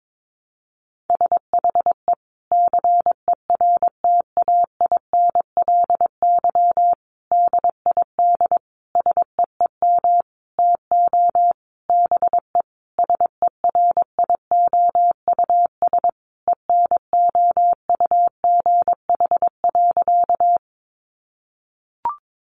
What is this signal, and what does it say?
Morse code, heard as a keyed tone, about 22 wpm, 700 Hz. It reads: SHE CERTAINLY DID SEEM TO BE SERIOUS ENOUGH.
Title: Sentences from Top 1100 Words 22wpm